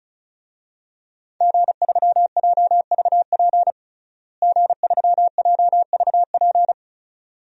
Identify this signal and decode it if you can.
Morse code, heard as a keyed tone, about 36 wpm, 700 Hz. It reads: G3JVP G3JVP
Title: DXCC-100 Call Signs - 5 Characters - Repeated 2x - 35wpm